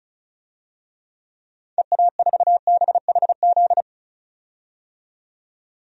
Morse code, heard as a keyed tone, about 35 wpm, 700 Hz. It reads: EA4BHZ
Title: DXCC-100 Call Signs - 6 Characters - 35wpm